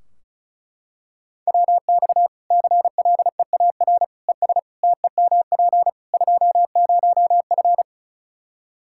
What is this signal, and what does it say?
Morse code, heard as a keyed tone, about 36 wpm, 700 Hz. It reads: WX CLEAR ES TEMP 20F
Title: K1USN SST QSO Practice 35wpm